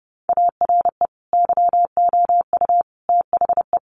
Morse code, heard as a keyed tone, about 30 wpm, 700 Hz. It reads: ARE YOU THE
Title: Sentences from Top 1300 Words 30wpm